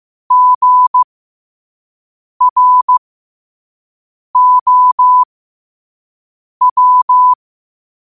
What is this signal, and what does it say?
Morse code, heard as a keyed tone, about 15 wpm, spaced as for 5 wpm, 1000 Hz. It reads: GROW
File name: qotdfiles / 5wpm